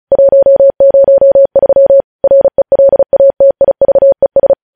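Morse code, heard as a keyed tone, about 36 wpm, 550 Hz. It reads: 103 RELATIVES
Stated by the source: News headlines sent in Morse at 35 WPM.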